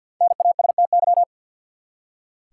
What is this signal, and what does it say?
Morse code, heard as a keyed tone, about 46 wpm, 700 Hz. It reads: NASTY
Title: Five Letter Words - 50wpm